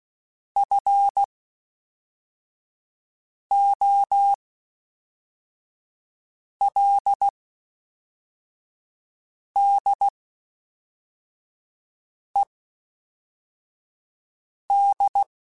Morse code, heard as a keyed tone, about 16 wpm, spaced as for 4 wpm, 800 Hz. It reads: FOLDED
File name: (spaced) / FromSMs spaced mp3s / Classes@5